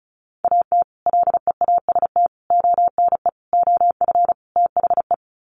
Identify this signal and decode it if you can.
Morse code, heard as a keyed tone, about 34 wpm, 700 Hz. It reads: AT LEAST ONE OF THE
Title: Sets of 5 Words Encore 35wpm